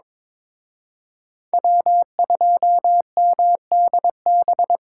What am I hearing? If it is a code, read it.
Morse code, heard as a keyed tone, about 25 wpm, 700 Hz. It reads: W2MDB
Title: Speed Racing - USA 1x3 Call Signs 22wpm